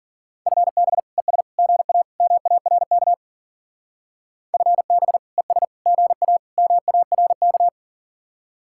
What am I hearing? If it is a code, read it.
Morse code, heard as a keyed tone, about 50 wpm, 700 Hz. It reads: FB ES GA MARK FB ES GA MARK
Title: Speed-Racing Common QSO Phrases - 40wpm